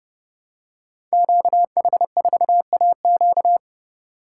Morse code, heard as a keyed tone, about 31 wpm, 700 Hz. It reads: QH4AQ
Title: Code Groups - 5 Letter:Numbers - 30wpm